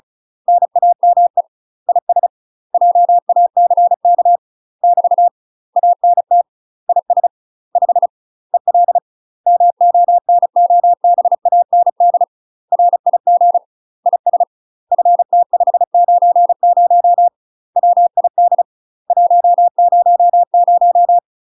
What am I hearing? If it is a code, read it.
Morse code, heard as a keyed tone, about 36 wpm, 700 Hz. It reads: NAME IS JACK <BT> ANT IS 5 EL MONOBAND RIG IS FT590 WID 100
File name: UK MASTER QSOs TRAINING / 35WPM / UK MASTER QSOs 0002